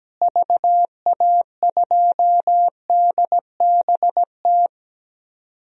Morse code, heard as a keyed tone, about 17 wpm, 700 Hz. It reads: VA2DBT
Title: DXCC-100 Call Signs - 6 Characters - Repeated 2x - 17wpm